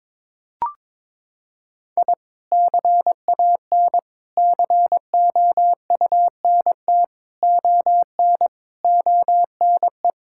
Morse code, heard as a keyed tone, about 21 wpm, 700 Hz. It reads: I CAN COUNT ON ONE
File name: Sets of 9 Words - 22wpm